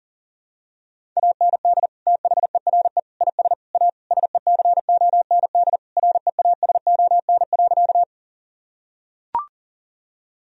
Morse code, heard as a keyed tone, about 40 wpm, 700 Hz. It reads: AND THERE IS A SECOND REASON.
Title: Sentences from Top 1000 Words - Review Part 1 - 40wpm